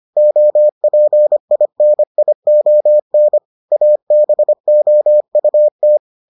Morse code, heard as a keyed tone, about 25 wpm, 600 Hz. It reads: OPINION ABOUT